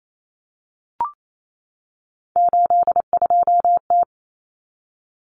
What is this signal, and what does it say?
Morse code, heard as a keyed tone, about 28 wpm, 700 Hz. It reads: 82T
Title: Common Cut Numbers - 3-Digit 28wpm